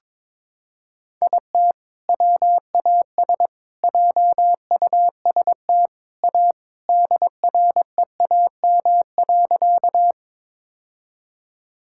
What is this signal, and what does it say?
Morse code, heard as a keyed tone, about 21 wpm, 700 Hz. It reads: IT WAS JUST A DREAM.
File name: practice / 22wpm / Sentences with 3-Word Phrases - 22wpm